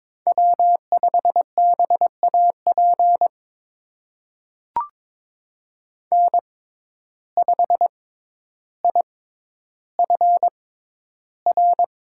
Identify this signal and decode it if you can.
Morse code, heard as a keyed tone, about 22 wpm, spaced as for 11 wpm, 700 Hz. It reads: W5BAP N5IFR
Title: USA Call Signs - 5 Characters - 6x Character Spacing - 22wpm